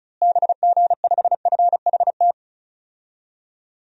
Morse code, heard as a keyed tone, about 34 wpm, 700 Hz. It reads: BG5FHT
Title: DXCC-100 Call Signs - 6 Characters - Repeated 3x - 35wpm